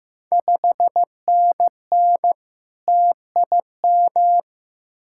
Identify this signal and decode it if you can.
Morse code, heard as a keyed tone, about 15 wpm, 690 Hz. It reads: 5NN TIM
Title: Ontario QSO Party Exchange 15wpm